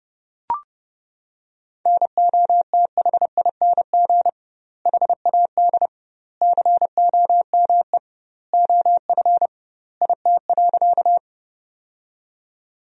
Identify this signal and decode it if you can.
Morse code, heard as a keyed tone, about 30 wpm, 700 Hz. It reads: NOTHING HAD COME OF IT.
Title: 2x Word-Spacing - Sentences from Top 500 Words - 30wpm